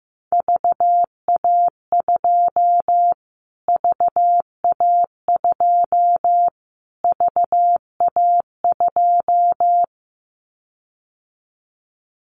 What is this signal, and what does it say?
Morse code, heard as a keyed tone, about 15 wpm, 700 Hz. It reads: VA2 VA2 VA2
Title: Callsign Prefixes for Top 25 DXCC - Repeated 3x - 15wpm